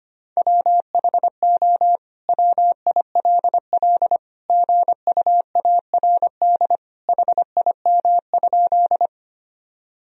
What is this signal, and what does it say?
Morse code, heard as a keyed tone, about 25 wpm, 700 Hz. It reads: WHO WILL GUARD HIM?
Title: Sentences from Top 1000 Words - Review Part 4 - 25wpm